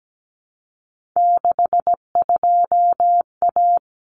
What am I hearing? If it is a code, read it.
Morse code, heard as a keyed tone, about 17 wpm, 700 Hz. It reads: 62A
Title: Common Cut Numbers - 3-Digit 17wpm